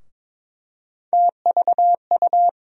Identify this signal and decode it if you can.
Morse code, heard as a keyed tone, about 22 wpm, 700 Hz. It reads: TVU
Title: USA Call Sign Postfixes - 3 Letters 22wpm